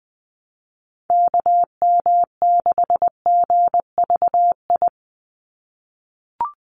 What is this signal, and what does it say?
Morse code, heard as a keyed tone, about 20 wpm, 700 Hz. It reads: KM6GVI
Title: USA Call Signs 20wpm